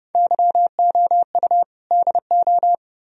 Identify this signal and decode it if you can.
Morse code, heard as a keyed tone, about 35 wpm, 700 Hz. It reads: YOU DO